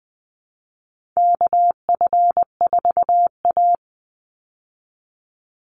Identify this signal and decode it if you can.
Morse code, heard as a keyed tone, about 20 wpm, 700 Hz. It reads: KF4A
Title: Call Signs Repeated 2x - 20wpm